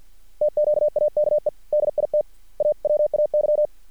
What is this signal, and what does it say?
Morse code, heard as a keyed tone, about 50 wpm, 600 Hz. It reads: TQAKE DIT AWAY
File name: archived / CRSnet